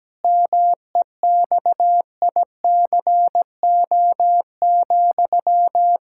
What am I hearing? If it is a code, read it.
Morse code, heard as a keyed tone, about 17 wpm, 700 Hz. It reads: MEXICO,
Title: Pangram Challenge v2 - 17wpm